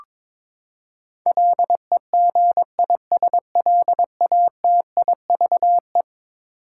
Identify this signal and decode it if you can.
Morse code, heard as a keyed tone, about 22 wpm, 700 Hz. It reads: LEGISLATIVE
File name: practice / 22wpm / Eleven Letter Words - 22wpm